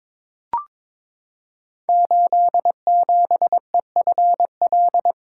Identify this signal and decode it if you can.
Morse code, heard as a keyed tone, about 22 wpm, 690 Hz. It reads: 87EFL
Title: Code Groups - 5 Letter:Numbers - 22wpm